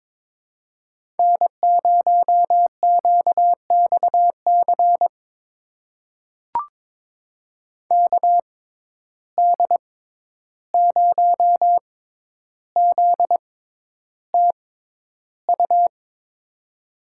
Morse code, heard as a keyed tone, about 22 wpm, spaced as for 11 wpm, 700 Hz. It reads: N0QXC KD0ZTU
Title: USA Call Signs - 6x Character Spacing - 22wpm